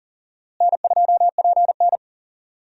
Morse code, heard as a keyed tone, about 40 wpm, 700 Hz. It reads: N2PN
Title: USA Call Signs - 4 Characters - Repeated 3x - 40wpm